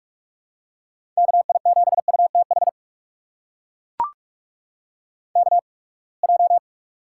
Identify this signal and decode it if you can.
Morse code, heard as a keyed tone, about 42 wpm, spaced as for 30 wpm, 700 Hz. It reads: KI7UTH KJ
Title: USA Call Signs - 6 Characters - 8x Character Spacing - 45wpm